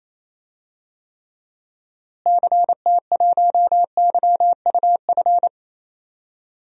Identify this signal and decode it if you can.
Morse code, heard as a keyed tone, about 28 wpm, 700 Hz. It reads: CT1YUF
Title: DXCC-100 Call Signs - 6 Characters - Repeated 3x - 28wpm